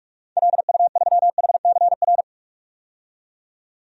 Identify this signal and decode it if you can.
Morse code, heard as a keyed tone, about 42 wpm, 700 Hz. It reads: LU3HCR
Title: DXCC-100 Call Signs - 6 Characters - 45wpm